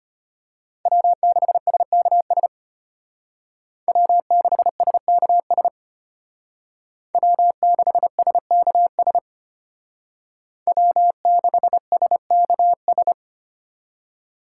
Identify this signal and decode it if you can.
Morse code, heard as a keyed tone, about 32 wpm, 700 Hz. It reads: W6SKS W6SKS W6SKS W6SKS